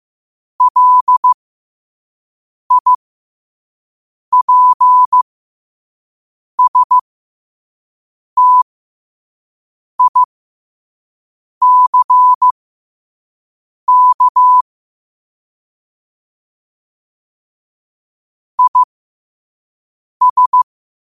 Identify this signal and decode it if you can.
Morse code, heard as a keyed tone, about 15 wpm, spaced as for 5 wpm, 1000 Hz. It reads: LIPSTICK IS
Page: New quotes every day in morse code at 5 Words per minute.